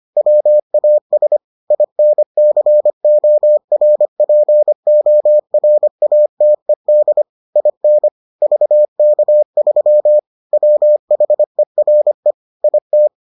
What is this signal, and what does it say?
Morse code, heard as a keyed tone, about 25 wpm, 600 Hz. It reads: WAS INCORPORATED IN VK3 WHERE IT